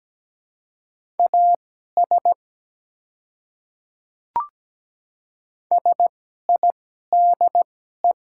Morse code, heard as a keyed tone, about 17 wpm, spaced as for 12 wpm, 700 Hz. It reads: AS SIDE
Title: 2x Character Spacing - Top 200 Words - 17wpm